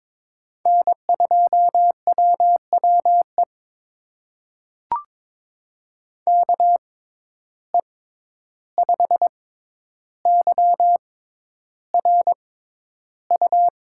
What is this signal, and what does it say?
Morse code, heard as a keyed tone, about 22 wpm, spaced as for 11 wpm, 700 Hz. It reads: N2WWE KE5YRU
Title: USA Call Signs - 6x Character Spacing - 22wpm